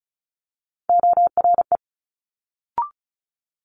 Morse code, heard as a keyed tone, about 36 wpm, 700 Hz. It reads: ORE
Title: Multi-letter Phonemes Quick Fire 35wpm